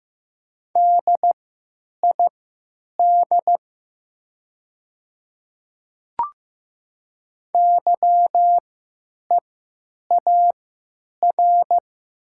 Morse code, heard as a keyed tone, about 15 wpm, spaced as for 9 wpm, 700 Hz. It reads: DID YEAR
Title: 3x Character Spacing - Top 500 Words - 15wpm